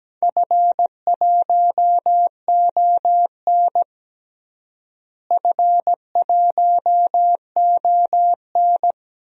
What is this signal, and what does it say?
Morse code, heard as a keyed tone, about 17 wpm, 700 Hz. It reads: F1ON F1ON
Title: DXCC-100 Call Signs - 4 Characters - Repeated 3x - 17wpm